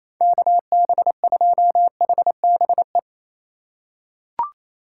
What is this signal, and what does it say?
Morse code, heard as a keyed tone, about 28 wpm, 700 Hz. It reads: KB2HBE
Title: USA Call Signs - 4x Character Spacing - 28wpm